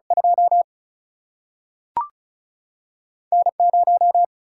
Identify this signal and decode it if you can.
Morse code, heard as a keyed tone, about 36 wpm, 700 Hz. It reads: 2 N 0
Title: Single-Letter Single-Number 35wpm